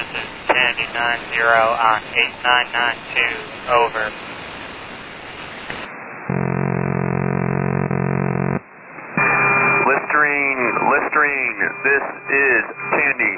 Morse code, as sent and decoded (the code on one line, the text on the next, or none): none